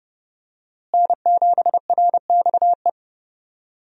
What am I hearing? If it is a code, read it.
Morse code, heard as a keyed tone, about 30 wpm, 700 Hz. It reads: N7RXE
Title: USA Call Signs - 5 Characters - Repeated 2x - 30wpm